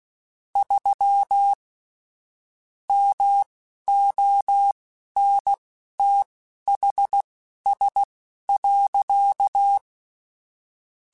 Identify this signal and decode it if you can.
Morse code, heard as a keyed tone, about 16 wpm, spaced as for 11 wpm, 800 Hz. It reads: 3 MONTHS.
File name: QSOs / QSOs-mp3s / WD0BGP 1@13